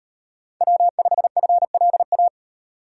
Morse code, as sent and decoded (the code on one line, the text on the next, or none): .-- ..... ..-. .-.. .-
W5FLA